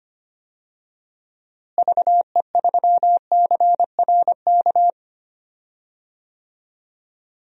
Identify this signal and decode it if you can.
Morse code, heard as a keyed tone, about 25 wpm, 700 Hz. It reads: VE3CRK
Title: DXCC-100 Call Signs - 6 Characters - Repeated 3x - 25wpm